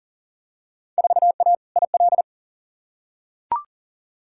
Morse code, heard as a keyed tone, about 40 wpm, 700 Hz. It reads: 4A IL
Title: Field Day Exchanges - 40wpm